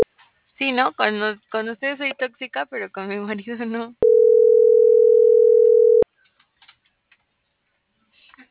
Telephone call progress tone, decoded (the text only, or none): ringback tone